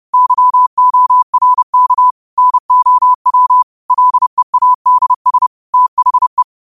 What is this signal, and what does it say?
Morse code, heard as a keyed tone, about 30 wpm, 1000 Hz. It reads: YORK NOW LEADS THE